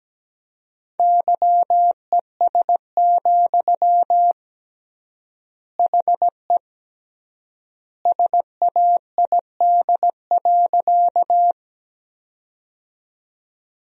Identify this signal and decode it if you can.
Morse code, heard as a keyed tone, about 17 wpm, 700 Hz. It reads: YES, HE SAID.
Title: 3x Word-Spacing - Sentences from Top 500 Words - 17wpm